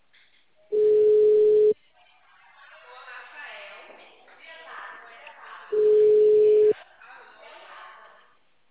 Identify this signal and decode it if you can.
Ringback tone (Europe, 425 Hz), with speech.
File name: out- -2044- - - .90602